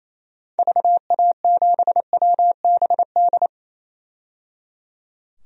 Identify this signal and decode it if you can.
Morse code, heard as a keyed tone, about 28 wpm, 700 Hz. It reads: VA7WBD
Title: DXCC-100 Call Signs - 6 Characters - 28wpm